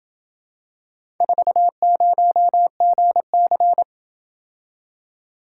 Morse code, heard as a keyed tone, about 27 wpm, 700 Hz. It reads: 40GC